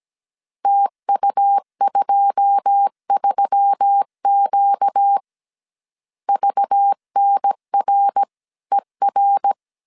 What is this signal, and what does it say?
Morse code, heard as a keyed tone, about 17 wpm, 800 Hz. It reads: TU23Q VNRER
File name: CW / DET / 1 9 1500C S C 16wpm 2909